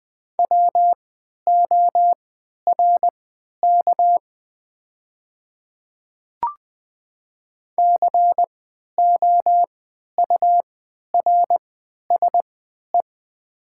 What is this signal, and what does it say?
Morse code, heard as a keyed tone, about 20 wpm, spaced as for 11 wpm, 700 Hz. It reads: WORK COURSE
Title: 3x Character Spacing - Top 500 Words - 20wpm